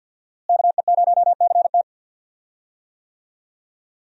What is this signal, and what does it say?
Morse code, heard as a keyed tone, about 50 wpm, 700 Hz. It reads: KE0CT
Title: USA Call Signs - 5 Characters - Repeated 3x - 50wpm